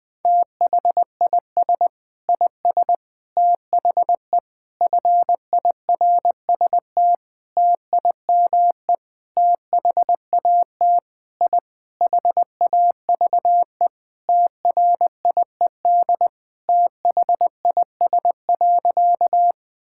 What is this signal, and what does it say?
Morse code, heard as a keyed tone, about 20 wpm, 700 Hz. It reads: THIS IS THE FIRST TIME THAT I HAVE TRIED THIS.
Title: Sentences with 6-Word Phrases - Spoken Beginning - 20wpm